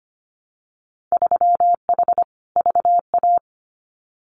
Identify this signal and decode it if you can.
Morse code, heard as a keyed tone, about 25 wpm, 700 Hz. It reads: 3H VA
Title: Winter Field Day Exchanges 25wpm